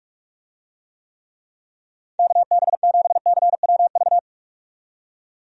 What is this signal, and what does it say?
Morse code, heard as a keyed tone, about 46 wpm, 700 Hz. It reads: KB7CWV
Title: USA Call Signs - 6 Characters - 45wpm